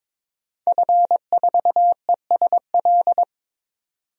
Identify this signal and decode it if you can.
Morse code, heard as a keyed tone, about 21 wpm, 700 Hz. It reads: F4ESL
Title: DXCC-100 Call Signs - 5 Characters - 22wpm